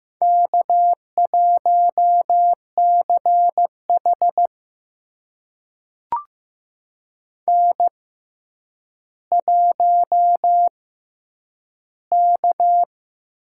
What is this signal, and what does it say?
Morse code, heard as a keyed tone, about 15 wpm, spaced as for 9 wpm, 700 Hz. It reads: K1CH N1K
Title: USA Call Signs - 6x Character Spacing - 15wpm